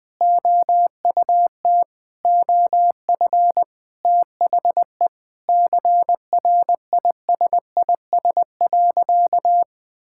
Morse code, heard as a keyed tone, about 20 wpm, 700 Hz. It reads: OUT OF THE CRISIS.
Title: Sentences with 8-Word Phrases - 20wpm